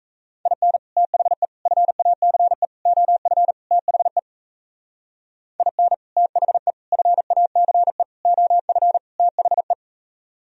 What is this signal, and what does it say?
Morse code, heard as a keyed tone, about 33 wpm, 700 Hz. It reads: IN THE FACE OF THE IN THE FACE OF THE
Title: Speed-Racing - Sets of 5 Words 28wpm